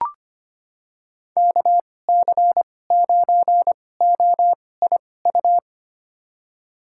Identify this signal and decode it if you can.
Morse code, heard as a keyed tone, about 25 wpm, spaced as for 18 wpm, 700 Hz. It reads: KC9OIU
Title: USA Call Signs - 6 Characters - 2x Character Spacing - 25wpm